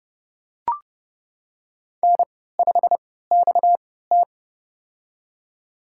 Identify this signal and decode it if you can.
Morse code, heard as a keyed tone, about 30 wpm, spaced as for 17 wpm, 700 Hz. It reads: N5XT
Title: USA Call Signs - 4 Characters - 3x Character Spacing - 30wpm